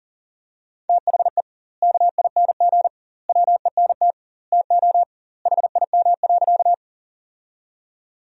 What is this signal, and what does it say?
Morse code, heard as a keyed tone, about 40 wpm, 700 Hz. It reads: THE KING WENT TO HIM.
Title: 2x Word-Spacing - Sentences from Top 400 Words - 40wpm